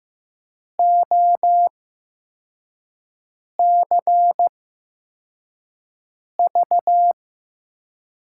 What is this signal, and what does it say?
Morse code, heard as a keyed tone, about 15 wpm, spaced as for 4 wpm, 700 Hz. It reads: OCV
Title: USA Call Signs - 8x Character Spacing - 15wpm